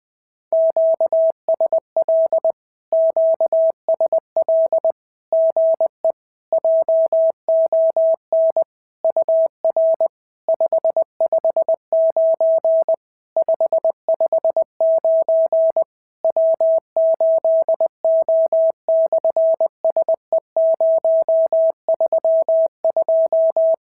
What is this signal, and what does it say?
Morse code, heard as a keyed tone, about 20 wpm, 650 Hz. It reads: QSL QSL GE JON UR 559 559 W8O/SE032
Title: SOTA QSO Practice 20wpm